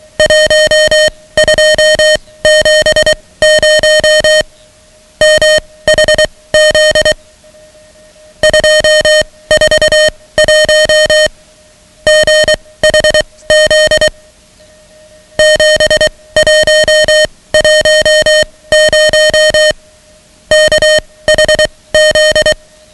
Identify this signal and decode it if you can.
Morse code, heard as a keyed tone, about 23 wpm, spaced as for 17 wpm, 620 Hz, with noise. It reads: 1270 MHZ 241 GHZ 7110 KHZ
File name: ham radio / CodeQuick / Code Quick 10